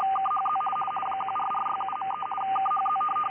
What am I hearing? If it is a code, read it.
RTTY, about 50 baud. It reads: KHZ 14467.3 KHZ
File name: IC-7100